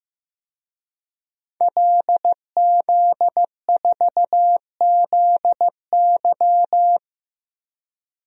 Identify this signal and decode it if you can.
Morse code, heard as a keyed tone, about 15 wpm, 700 Hz. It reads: LZ4ZY
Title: DXCC-100 Call Signs - 5 Characters - 15wpm